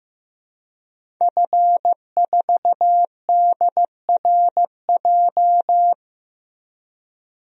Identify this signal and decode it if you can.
Morse code, heard as a keyed tone, about 15 wpm, 700 Hz. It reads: F4DRJ
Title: DXCC-100 Call Signs - 5 Characters - Repeated 3x - 15wpm